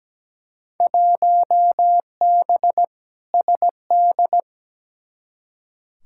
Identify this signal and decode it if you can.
Morse code, heard as a keyed tone, about 17 wpm, 700 Hz. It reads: 1B SD
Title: Field Day Exchanges Encore - 17wpm